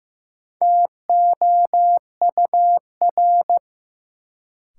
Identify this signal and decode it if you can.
Morse code, heard as a keyed tone, about 15 wpm, 700 Hz. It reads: TOUR
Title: Sentences from Top 1400 Words - New Word Drill Alt - 15wpm